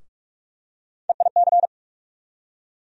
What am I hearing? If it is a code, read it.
Morse code, heard as a keyed tone, about 42 wpm, 700 Hz. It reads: EIC
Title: USA Call Sign Postfixes - 3 Letters 45wpm